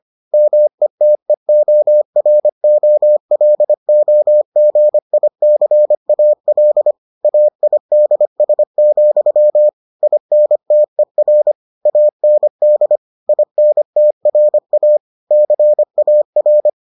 Morse code, heard as a keyed tone, about 25 wpm, 600 Hz. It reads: METEOROLOGICAL AIDS, INTER AND INTRA CAR